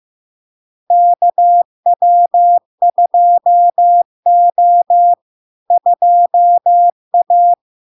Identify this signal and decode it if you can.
Morse code, heard as a keyed tone, about 15 wpm, 700 Hz. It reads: KW2O 2A
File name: LARK-FD-2022-Log-in-CW-15wpm-700-Hz